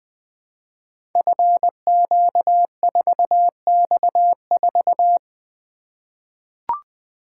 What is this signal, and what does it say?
Morse code, heard as a keyed tone, about 20 wpm, 700 Hz. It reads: FQ4X4
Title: Code Groups - 5 Letter:Numbers - 20wpm